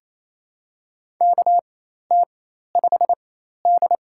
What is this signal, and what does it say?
Morse code, heard as a keyed tone, about 28 wpm, spaced as for 13 wpm, 700 Hz. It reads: KT5D
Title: USA Call Signs - 4 Characters - 4x Character Spacing - 28wpm